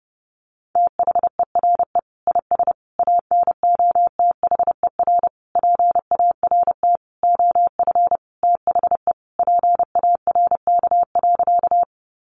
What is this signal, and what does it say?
Morse code, heard as a keyed tone, about 30 wpm, 700 Hz. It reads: THERE IS ANOTHER PART OF THE PARK.
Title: Sentences from Top 1300 Words 30wpm